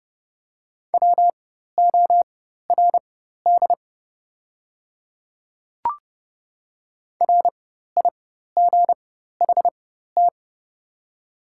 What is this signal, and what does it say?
Morse code, heard as a keyed tone, about 29 wpm, spaced as for 14 wpm, 700 Hz. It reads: WORD RIGHT
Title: 4x Character Spacing - Top 500 Words - 30wpm